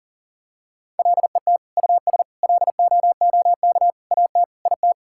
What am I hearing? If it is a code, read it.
Morse code, heard as a keyed tone, about 40 wpm, 700 Hz. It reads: LET US LOOK AT IT
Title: Sentences from Top 1000 Words - Review Part 6 - 40wpm